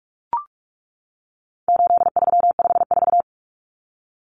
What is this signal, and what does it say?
Morse code, heard as a keyed tone, about 44 wpm, 700 Hz. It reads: 8354
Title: Numbers - 4-Digit 45wpm